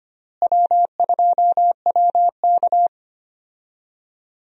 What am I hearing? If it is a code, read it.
Morse code, heard as a keyed tone, about 25 wpm, 700 Hz. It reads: W2WK
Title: USA Call Signs - 4 Characters - Repeated 3x - 25wpm